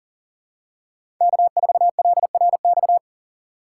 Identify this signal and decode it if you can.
Morse code, heard as a keyed tone, about 40 wpm, 700 Hz. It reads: K4LRX
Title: K1USN SST QSO Practice - 40wpm